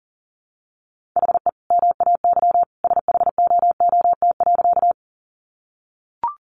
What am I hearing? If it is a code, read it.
Morse code, heard as a keyed tone, about 40 wpm, 700 Hz. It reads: HE MAY SHOOT.
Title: Sentences from Top 1200 Words 40wpm